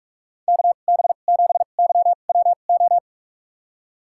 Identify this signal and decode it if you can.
Morse code, heard as a keyed tone, about 46 wpm, spaced as for 32 wpm, 700 Hz. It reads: KB7YWO
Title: USA Call Signs - 2x Character Spacing - 45wpm